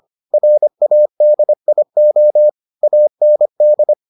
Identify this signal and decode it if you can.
Morse code, heard as a keyed tone, about 25 wpm, 600 Hz. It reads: RADIO AND